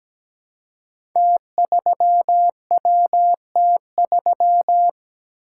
Morse code, heard as a keyed tone, about 17 wpm, 700 Hz. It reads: T3WT3
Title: Code Groups - 5 Letter:Numbers - 17wpm